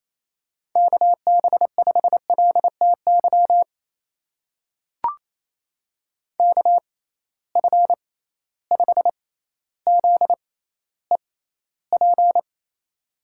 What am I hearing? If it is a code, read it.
Morse code, heard as a keyed tone, about 28 wpm, spaced as for 14 wpm, 700 Hz. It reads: KB5LTY KF5ZEP
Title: USA Call Signs - 6 Characters - 6x Character Spacing - 28wpm